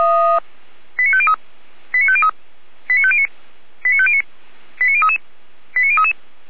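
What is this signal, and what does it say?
Five-tone selective call (ZVEI), heard as quick runs of five tones, tones of about 70 ms. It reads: 89503 89503 89509 89509 89033 89033
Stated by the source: ZVEI five-tone selective call received on an SDR.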